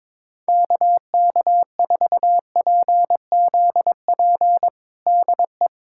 Morse code, heard as a keyed tone, about 22 wpm, 700 Hz. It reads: KK4PZP DE
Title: USA Callsign DE Callsign - 6 Characters - 22wpm